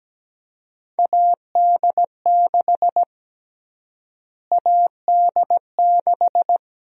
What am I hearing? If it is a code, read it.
Morse code, heard as a keyed tone, about 17 wpm, 700 Hz. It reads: AD6 AD6
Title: USA Call Sign Prefixes - Repeated 2x - 17wpm